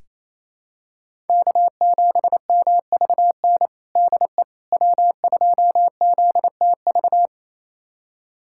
Morse code, heard as a keyed tone, about 28 wpm, 700 Hz. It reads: K7MVN DE W2ZTV
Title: USA Callsign DE Callsign 28wpm